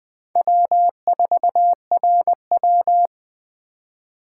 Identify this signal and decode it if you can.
Morse code, heard as a keyed tone, about 19 wpm, 700 Hz. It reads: W4RW